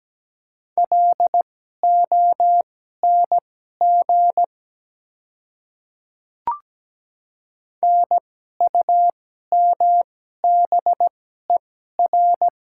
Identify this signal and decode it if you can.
Morse code, heard as a keyed tone, about 17 wpm, spaced as for 12 wpm, 700 Hz. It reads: LONG NUMBER
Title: 2x Character Spacing - Top 100 Words - 17wpm